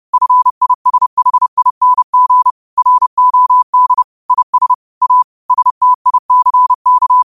Morse code, heard as a keyed tone, about 30 wpm, 1000 Hz. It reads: FISHING ROD IS A STICK
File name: qotdfiles / 30wpm